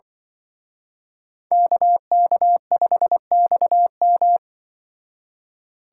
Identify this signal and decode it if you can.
Morse code, heard as a keyed tone, about 24 wpm, 700 Hz. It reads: KK5XM
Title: Speed Racing - USA 2x2 Call Signs 20wpm